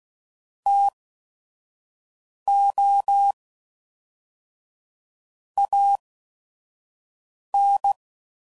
Morse code, heard as a keyed tone, about 16 wpm, spaced as for 5 wpm, 800 Hz. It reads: TO AN
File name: CW Code Groups @5WPM / Technical Subjects / XL